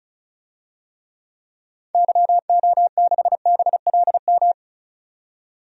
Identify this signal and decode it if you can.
Morse code, heard as a keyed tone, about 36 wpm, 700 Hz. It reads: YO6BLM
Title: DXCC-100 Call Signs - 6 Characters - 35wpm